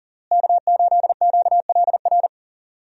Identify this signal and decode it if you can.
Morse code, heard as a keyed tone, about 40 wpm, 700 Hz. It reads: K8QLR
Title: USA Call Signs - 5 Characters - Repeated 3x - 40wpm